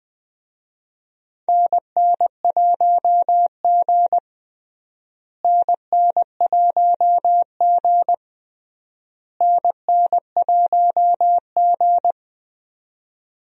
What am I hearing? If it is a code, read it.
Morse code, heard as a keyed tone, about 21 wpm, 700 Hz. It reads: NN1G NN1G NN1G
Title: USA Call Signs - 4 Characters - Repeated 3x - 20wpm